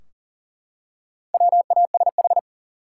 Morse code, heard as a keyed tone, about 40 wpm, 700 Hz. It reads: WASH
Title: Four Letter Words - 40wpm